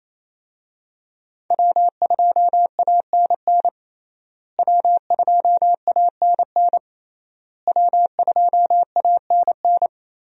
Morse code, heard as a keyed tone, about 28 wpm, 700 Hz. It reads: W2ANN W2ANN W2ANN
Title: USA Call Signs - 5 Characters - Repeated 3x - 28wpm